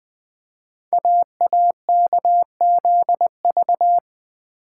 Morse code, heard as a keyed tone, about 21 wpm, 700 Hz. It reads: AAKZV
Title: Code Groups - 5 Letter:Numbers - 20wpm